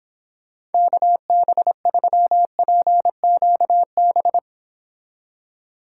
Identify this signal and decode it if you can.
Morse code, heard as a keyed tone, about 29 wpm, 700 Hz. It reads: KB3PQB